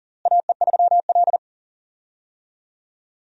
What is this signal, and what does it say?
Morse code, heard as a keyed tone, about 37 wpm, 700 Hz. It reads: AE3L
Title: USA Call Signs - 4 Characters - Repeated 3x - 40wpm